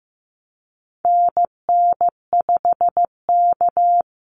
Morse code, heard as a keyed tone, about 15 wpm, 700 Hz. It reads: NN5K
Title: NA SOTA Activator Callsigns 15wpm